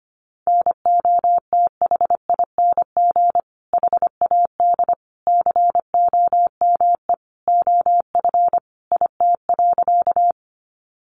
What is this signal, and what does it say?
Morse code, heard as a keyed tone, about 25 wpm, 700 Hz. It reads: NOTHING HAD COME OF IT.
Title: Sentences from Top 500 Words 25wpm